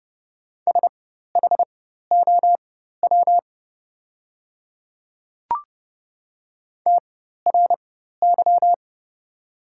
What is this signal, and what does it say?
Morse code, heard as a keyed tone, about 30 wpm, spaced as for 14 wpm, 700 Hz. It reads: SHOW TRY
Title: 4x Character Spacing - Top 200 Words - 30wpm